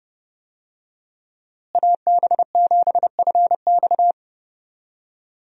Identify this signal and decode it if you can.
Morse code, heard as a keyed tone, about 30 wpm, 700 Hz. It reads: AB7FX
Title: USA Call Signs - 5 Characters - Repeated 3x - 30wpm